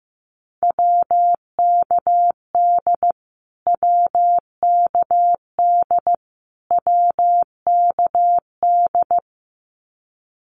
Morse code, heard as a keyed tone, about 15 wpm, 700 Hz. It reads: WKD WKD WKD
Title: Abbreviations Repeated 3x - 15wpm